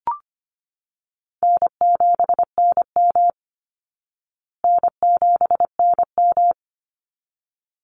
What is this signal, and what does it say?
Morse code, heard as a keyed tone, about 25 wpm, 700 Hz. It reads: N7NM N7NM
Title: Easy Call Signs Repeated 2x - 25wpm